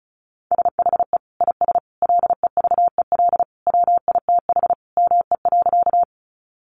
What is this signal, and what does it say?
Morse code, heard as a keyed tone, about 34 wpm, 700 Hz. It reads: SHE IS LEVEL WITH ME.
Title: Sentences from Top 700 Words 35wpm